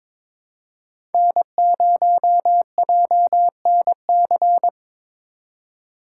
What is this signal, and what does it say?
Morse code, heard as a keyed tone, about 26 wpm, 700 Hz. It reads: N0JNC
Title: Speed Racing - USA Call Signs 22wpm